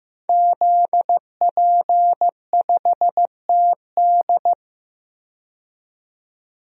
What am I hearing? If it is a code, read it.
Morse code, heard as a keyed tone, about 16 wpm, 700 Hz. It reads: ZP5TD
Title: DXCC-100 Call Signs - 5 Characters - 15wpm